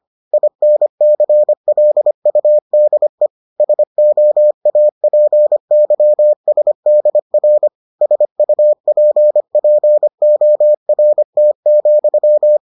Morse code, heard as a keyed tone, about 25 wpm, 600 Hz. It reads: INCLUDE SOAPYSDR SUPPORT,